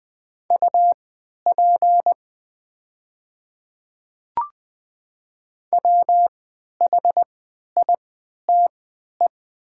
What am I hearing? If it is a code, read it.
Morse code, heard as a keyed tone, about 20 wpm, spaced as for 11 wpm, 690 Hz. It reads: UP WHITE
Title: 3x Character Spacing - Top 300 Words - 20wpm